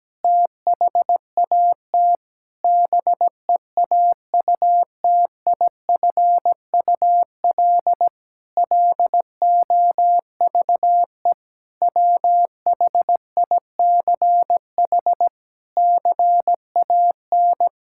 Morse code, heard as a keyed tone, about 17 wpm, 700 Hz. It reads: THAT BEAUTIFUL LOVE WHICH CAN
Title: Sentences from Top 1600 Words 17wpm